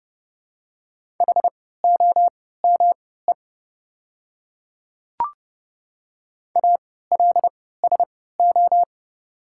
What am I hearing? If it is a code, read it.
Morse code, heard as a keyed tone, about 30 wpm, spaced as for 17 wpm, 700 Hz. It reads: HOME ALSO
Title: 3x Character Spacing - Top 200 Words - 30wpm